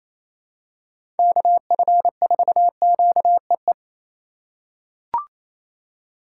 Morse code, heard as a keyed tone, about 28 wpm, 700 Hz. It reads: KF4QEE
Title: USA Call Signs - 6 Characters - 2x Character Spacing - 28wpm